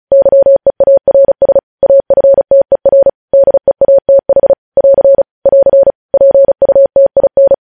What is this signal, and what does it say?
Morse code, heard as a keyed tone, about 36 wpm, 550 Hz. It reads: YEARS AFTER DEATH <AR> <AR> PUTIN
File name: podcast / News headlines-1x-35x35